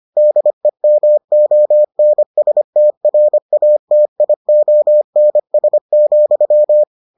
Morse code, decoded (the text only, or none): DEMONSTRATIONS,